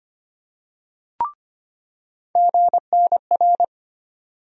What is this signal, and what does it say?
Morse code, heard as a keyed tone, about 25 wpm, 700 Hz. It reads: GNR
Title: USA Call Sign Postfixes - 3 Letters 25wpm